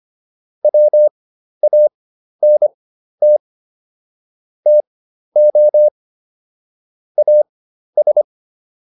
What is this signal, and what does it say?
Morse code, heard as a keyed tone, about 25 wpm, spaced as for 12 wpm, 600 Hz. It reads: WANT TO AS